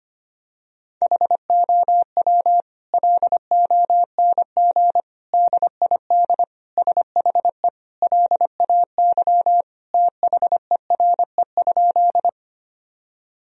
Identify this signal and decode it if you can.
Morse code, heard as a keyed tone, about 25 wpm, 700 Hz. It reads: HOW LONG DID SHE LAY THERE?
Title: Sentences from Top 1000 Words - Review Part 2 - 25wpm